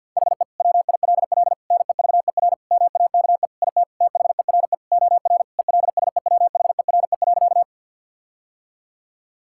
Morse code, heard as a keyed tone, about 50 wpm, 700 Hz. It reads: HE WILL NEVER MAKE IT THERE OR ELSEWHERE.
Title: Sentences from Top 1600 Words 50wpm